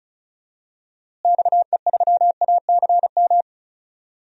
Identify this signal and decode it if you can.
Morse code, heard as a keyed tone, about 36 wpm, 700 Hz. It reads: XE3ACM
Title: DXCC-100 Call Signs - 6 Characters - Repeated 3x - 35wpm